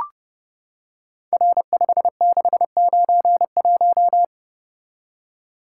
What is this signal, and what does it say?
Morse code, heard as a keyed tone, about 30 wpm, 700 Hz. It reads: R5691
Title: Code Groups - 5 Letter:Numbers - 30wpm